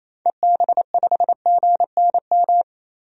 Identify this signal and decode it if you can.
Morse code, heard as a keyed tone, about 28 wpm, 700 Hz. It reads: EB5GNM
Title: DXCC-100 Call Signs - 6 Characters - 28wpm